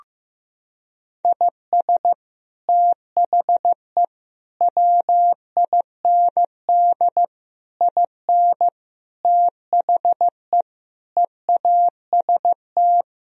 Morse code, heard as a keyed tone, about 15 wpm, 700 Hz. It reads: IS THE WIND IN THE EAST
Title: Sentences from Top 1000 Words - Review Part 3 - 15wpm